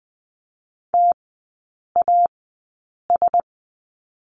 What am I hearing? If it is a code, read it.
Morse code, heard as a keyed tone, about 20 wpm, spaced as for 8 wpm, 700 Hz. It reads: TAS
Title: 093 Lesson 16e - Letter c - Three Characters 20wpm